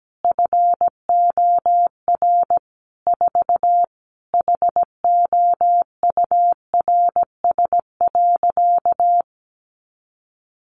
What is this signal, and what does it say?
Morse code, heard as a keyed tone, about 17 wpm, 700 Hz. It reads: FOR 4 HOURS.
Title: Additional Easy Phrases 17wpm